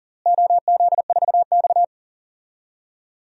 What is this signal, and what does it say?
Morse code, heard as a keyed tone, about 39 wpm, 700 Hz. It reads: OZ4X
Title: DXCC-100 Call Signs - 4 Characters - Repeated 3x - 40wpm